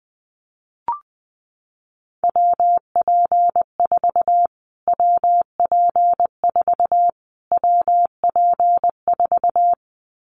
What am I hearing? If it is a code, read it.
Morse code, heard as a keyed tone, about 20 wpm, 700 Hz. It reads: WP4 WP4 WP4
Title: Callsign Prefixes for Top 25 DXCC - Repeated 3x - 20wpm